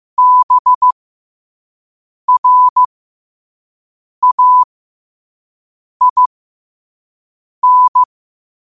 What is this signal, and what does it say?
Morse code, heard as a keyed tone, about 15 wpm, spaced as for 5 wpm, 1000 Hz. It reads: BRAIN
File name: qotdfiles / 5wpm